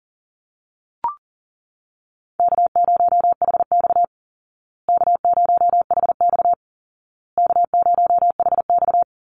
Morse code, heard as a keyed tone, about 40 wpm, 700 Hz. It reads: K0HX K0HX K0HX
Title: Call Signs Repeated 2x - 40wpm